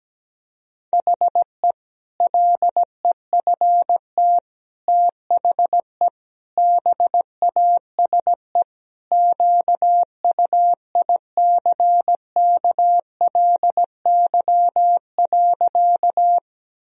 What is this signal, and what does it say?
Morse code, heard as a keyed tone, about 17 wpm, 700 Hz. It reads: HE LEFT THE BASE QUICKLY.
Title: Sentences from Top 1000 Words - Review Part 1 - 17wpm